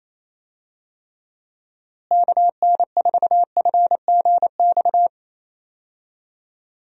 Morse code, heard as a keyed tone, about 28 wpm, 700 Hz. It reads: KN4FGX
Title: USA Call Signs - 6 Characters - Repeated 2x - 28wpm